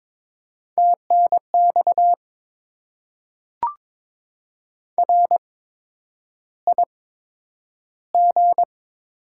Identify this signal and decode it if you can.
Morse code, heard as a keyed tone, about 22 wpm, spaced as for 9 wpm, 700 Hz. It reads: TNX RIG
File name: Top 20 QSO Elements - 8x Character Spacing - 22wpm